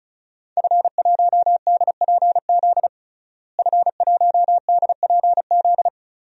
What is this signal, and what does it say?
Morse code, heard as a keyed tone, about 34 wpm, 700 Hz. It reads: F1DPZ F1DPZ
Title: DXCC-100 Call Signs - 5 Characters - Repeated 3x - 35wpm